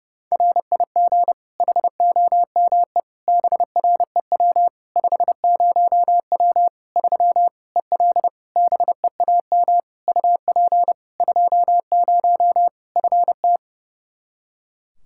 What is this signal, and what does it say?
Morse code, heard as a keyed tone, about 32 wpm, 700 Hz. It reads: RIG HOME BREW 50W 3 EL BEAM UP 20 FT
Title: Practice QSO - Exchange 2 - Info - 30wpm